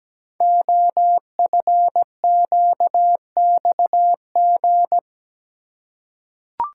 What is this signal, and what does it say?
Morse code, heard as a keyed tone, about 17 wpm, 700 Hz. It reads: OFQXG
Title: Code Groups - 5 Letters - 17wpm